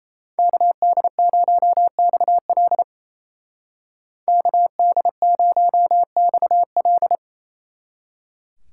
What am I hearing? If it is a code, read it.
Morse code, heard as a keyed tone, about 34 wpm, 700 Hz. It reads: KD0XL KD0XL
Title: Speed Racing - USA 2x2 Call Signs 28wpm